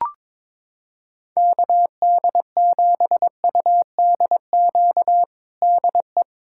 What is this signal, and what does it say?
Morse code, heard as a keyed tone, about 22 wpm, 700 Hz. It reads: KD7UDQ DE
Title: USA Callsign DE Callsign - 6 Characters - 22wpm